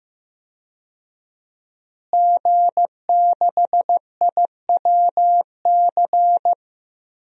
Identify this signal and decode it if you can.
Morse code, heard as a keyed tone, about 15 wpm, 700 Hz. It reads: G6IWC
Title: DXCC-100 Call Signs - 5 Characters - Repeated 3x - 15wpm